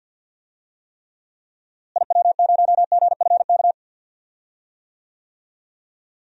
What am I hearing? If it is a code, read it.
Morse code, heard as a keyed tone, about 46 wpm, 700 Hz. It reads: IW0GFK
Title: DXCC-100 Call Signs - 6 Characters - Repeated 2x - 50wpm